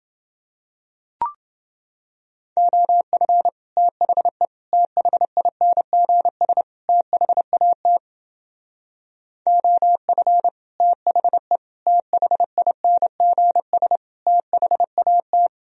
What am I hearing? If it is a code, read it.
Morse code, heard as a keyed tone, about 29 wpm, 700 Hz. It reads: OF THE THINGS THAT OF THE THINGS THAT